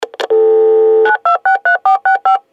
DTMF, heard as telephone touch tones, tones of about 120 ms. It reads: 9263465
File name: mf SE-7815-computer modem 1